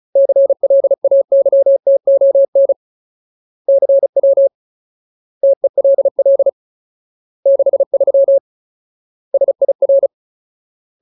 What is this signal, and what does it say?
Morse code, heard as a keyed tone, about 34 wpm, 550 Hz. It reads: CLAYTON CW TELL 63 SIR
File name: Ultimate Word List / 35WPM Ultimate